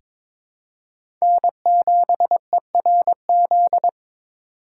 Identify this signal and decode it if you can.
Morse code, heard as a keyed tone, about 22 wpm, 700 Hz. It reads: N7ERZ
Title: USA Call Signs - 5 Characters - 3x Character Spacing - 22wpm